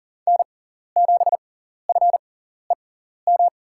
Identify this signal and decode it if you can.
Morse code, heard as a keyed tone, about 40 wpm, spaced as for 14 wpm, 700 Hz. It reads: N7FEM
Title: USA Call Signs - 5 Characters - 6x Character Spacing - 40wpm